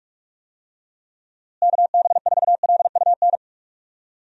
Morse code, heard as a keyed tone, about 41 wpm, 700 Hz. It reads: KB4LUN
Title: USA Call Signs - 6 Characters - Repeated 2x - 45wpm